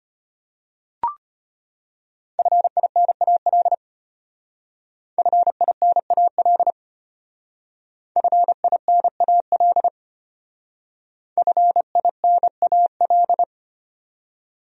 Morse code, heard as a keyed tone, about 32 wpm, 700 Hz. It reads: FINAL FINAL FINAL FINAL